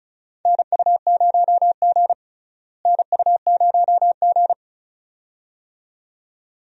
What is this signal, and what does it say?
Morse code, heard as a keyed tone, about 36 wpm, 700 Hz. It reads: NU0G NU0G
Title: USA Call Signs - 4 Characters - Repeated 2x - 35wpm